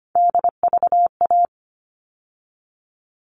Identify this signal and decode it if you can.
Morse code, heard as a keyed tone, about 25 wpm, 700 Hz. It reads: DVA
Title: All Cut Numbers - 3-Digit 25wpm